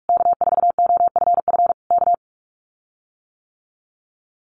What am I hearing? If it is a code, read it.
Morse code, heard as a keyed tone, about 44 wpm, 700 Hz. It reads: K4OFF K
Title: Common QSO Phrases Encore 45wpm